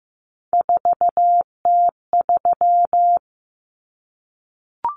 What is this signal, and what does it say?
Morse code, heard as a keyed tone, about 15 wpm, 700 Hz. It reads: 4T3
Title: Common Cut Numbers - 3-Digit 15wpm